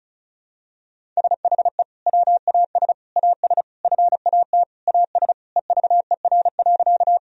Morse code, heard as a keyed tone, about 36 wpm, 700 Hz. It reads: SHE WAS AS FAT AS EVER.
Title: Sentences from Top 1000 Words - Review Part 4 - 35wpm